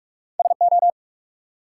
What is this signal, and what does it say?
Morse code, heard as a keyed tone, about 58 wpm, 700 Hz. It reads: SO